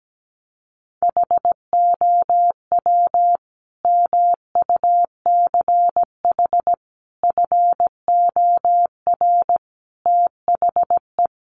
Morse code, heard as a keyed tone, about 17 wpm, 700 Hz. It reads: HOW MUCH FOR THE
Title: Sentences from Top 1200 Words 17wpm